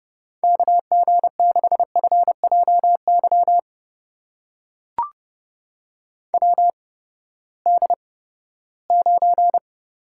Morse code, heard as a keyed tone, about 30 wpm, spaced as for 17 wpm, 700 Hz. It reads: KG6FJY WD9
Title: USA Call Signs - 6 Characters - 8x Character Spacing - 30wpm